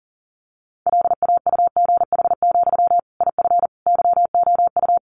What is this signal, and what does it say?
Morse code, heard as a keyed tone, about 40 wpm, 700 Hz. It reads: LAUGH, IF YOU
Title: Sentences from Top 500 Words 40wpm